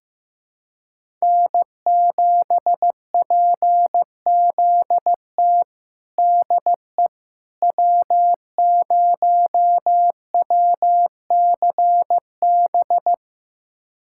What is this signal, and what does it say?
Morse code, heard as a keyed tone, about 15 wpm, 700 Hz. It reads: N7PZT DE W0WCB
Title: USA Callsign DE Callsign 15wpm